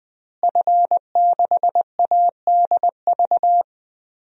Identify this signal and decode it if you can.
Morse code, heard as a keyed tone, about 21 wpm, 700 Hz. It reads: F6ADV
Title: DXCC-100 Call Signs - 5 Characters - Repeated 3x - 20wpm